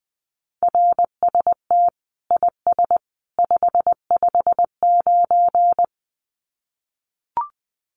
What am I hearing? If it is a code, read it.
Morse code, heard as a keyed tone, about 20 wpm, 700 Hz. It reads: RST IS 559
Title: Common QSO Phrases 20wpm